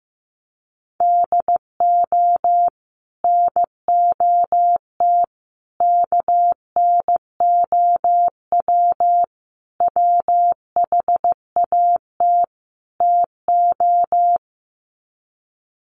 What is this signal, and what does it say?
Morse code, heard as a keyed tone, about 15 wpm, 700 Hz. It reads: DO NOT KNOW WHAT TO
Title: Sets of 5 Words 15wpm